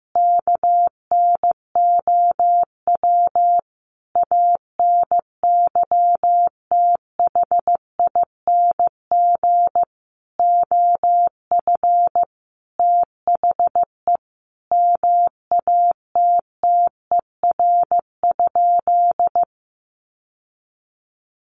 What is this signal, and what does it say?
Morse code, heard as a keyed tone, about 15 wpm, 700 Hz. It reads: KNOW ANYTHING OF THE MATTER?
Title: Sentences from Top 1000 Words 15wpm